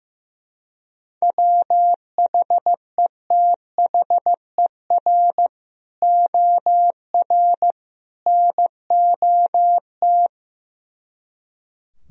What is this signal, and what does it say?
Morse code, heard as a keyed tone, about 15 wpm, 700 Hz. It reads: WHETHER OR NOT
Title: Speed-Racing - Sets of 3 Words 15wpm